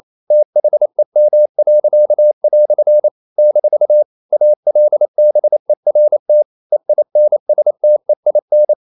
Morse code, heard as a keyed tone, about 28 wpm, 600 Hz. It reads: THEM." - ALBERT EINSTEIN
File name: CW / Practice / EnglishQuotations0228